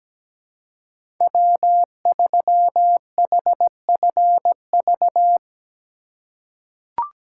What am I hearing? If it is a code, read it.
Morse code, heard as a keyed tone, about 17 wpm, 700 Hz. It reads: W3HFV
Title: USA Call Signs - 5 Characters - 2x Character Spacing - 17wpm